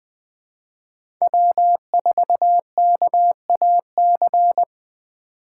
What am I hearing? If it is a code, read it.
Morse code, heard as a keyed tone, about 20 wpm, 700 Hz. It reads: W4KAC
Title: 2023 SOTA Chaser Call Signs - 20wpm